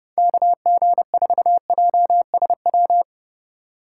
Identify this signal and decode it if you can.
Morse code, heard as a keyed tone, about 30 wpm, 700 Hz. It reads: KG4JSW
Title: USA Call Signs - 6x Character Spacing - 30wpm